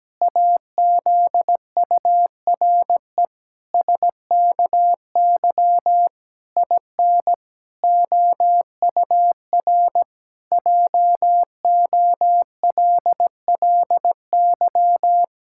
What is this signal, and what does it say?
Morse code, heard as a keyed tone, about 17 wpm, 700 Hz. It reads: AZURE SKY IN OUR JOLLY
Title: Pangram Challenge v2 - 17wpm